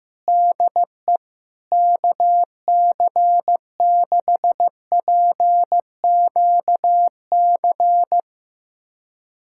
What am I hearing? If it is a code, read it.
Morse code, heard as a keyed tone, about 15 wpm, 700 Hz. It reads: DE KC6PQC
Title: USA Callsign DE Callsign 15wpm